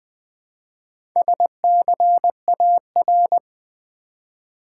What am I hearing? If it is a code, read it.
Morse code, heard as a keyed tone, about 20 wpm, 700 Hz. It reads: SCAR
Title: Build A Word - 1 Letter At A Time - 5-Letter Words - 20wpm